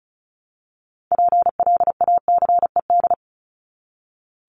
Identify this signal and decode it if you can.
Morse code, heard as a keyed tone, about 34 wpm, 700 Hz. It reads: PLACED
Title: ED Suffix - Top 100 Words with Base Words 35wpm